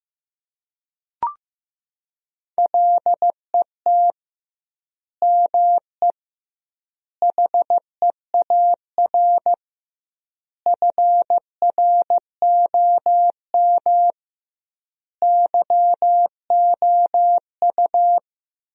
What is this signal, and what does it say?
Morse code, heard as a keyed tone, about 15 wpm, 700 Hz. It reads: LET ME HEAR FROM YOU
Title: 2x Word-Spacing - Sentences from Top 400 Words - 15wpm